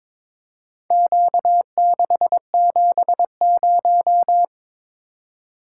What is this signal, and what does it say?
Morse code, heard as a keyed tone, about 23 wpm, 700 Hz. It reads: Q670
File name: practice / 22wpm / Four Letter-Number Combinations - 22wpm